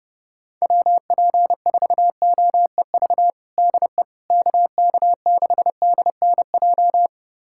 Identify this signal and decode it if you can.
Morse code, heard as a keyed tone, about 30 wpm, 700 Hz. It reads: WP4OEV DE KK6DNJ
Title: USA Callsign DE Callsign - 6 Characters - 30wpm